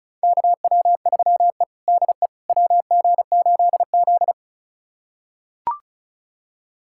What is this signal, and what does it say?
Morse code, heard as a keyed tone, about 34 wpm, 700 Hz. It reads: KW3E DE WG8Z
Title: USA Callsign DE Callsign - 4 Characters - 35wpm